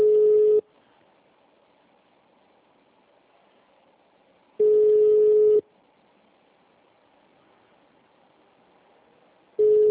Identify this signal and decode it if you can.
Ringback tone (Europe, 425 Hz).
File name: out- -2006- - - .207